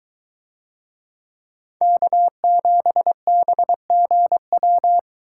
Morse code, heard as a keyed tone, about 22 wpm, 700 Hz. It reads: K7BGW